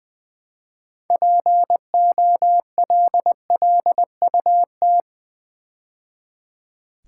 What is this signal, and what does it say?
Morse code, heard as a keyed tone, about 20 wpm, 700 Hz. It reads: POLLUT
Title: Build A Word - 3 Letters At A Time - 9-Letter Words - 20wpm